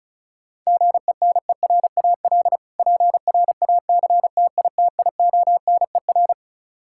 Morse code, heard as a keyed tone, about 36 wpm, 700 Hz. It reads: GENERAL PRACTITIONER
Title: Practice QSO - Exchange 3 - More Info - 35wpm